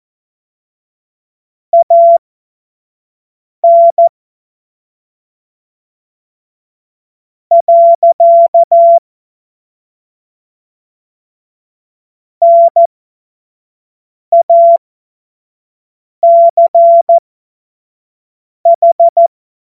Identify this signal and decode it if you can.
Morse code, heard as a keyed tone, about 14 wpm, spaced as for 4 wpm, 680 Hz. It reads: AN. NACH
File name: Morsekurs / Witze / Witz 14. September 2021, 25 F70